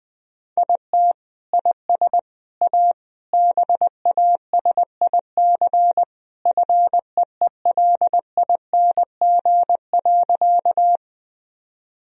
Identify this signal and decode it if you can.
Morse code, heard as a keyed tone, about 20 wpm, 700 Hz. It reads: IT IS A BASIC FEELING.
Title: Sentences from Top 1000 Words - Review Part 3 - 20wpm